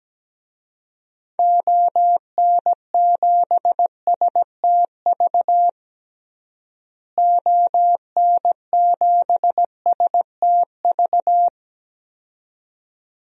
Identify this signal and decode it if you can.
Morse code, heard as a keyed tone, about 17 wpm, 700 Hz. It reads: ON7STV ON7STV
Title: DXCC-100 Call Signs - 6 Characters - Repeated 2x - 17wpm